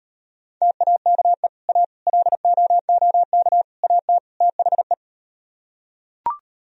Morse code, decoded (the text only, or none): TAKE A LOOK AT THE